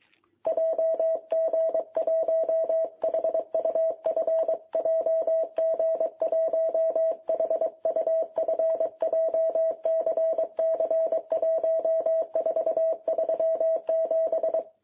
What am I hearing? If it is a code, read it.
Morse code, heard as a keyed tone, about 23 wpm, 660 Hz. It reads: JG1HUF JG1HUF JCC 1437
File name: jh8cbh / 2021cw